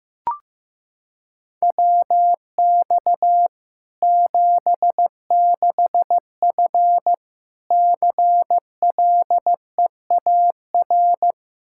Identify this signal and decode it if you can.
Morse code, heard as a keyed tone, about 15 wpm, 700 Hz. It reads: WX 76F CLEAR
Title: K1USN SST QSO Practice 15wpm